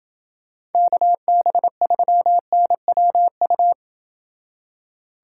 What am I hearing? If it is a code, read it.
Morse code, heard as a keyed tone, about 29 wpm, 700 Hz. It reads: KB3NWU